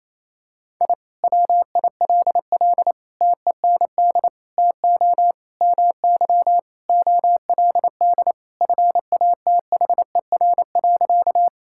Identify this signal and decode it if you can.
Morse code, heard as a keyed tone, about 28 wpm, 700 Hz. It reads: I WILL TEND TO MY OLD FATHER.
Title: Sentences from Top 1000 Words - Review Part 5 - 28wpm